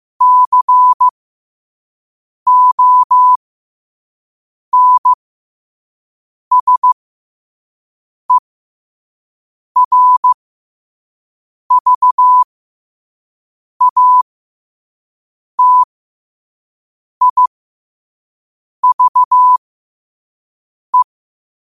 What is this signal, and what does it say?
Morse code, heard as a keyed tone, about 15 wpm, spaced as for 5 wpm, 1000 Hz. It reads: CONSERVATIVE
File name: qotdfiles / 5wpm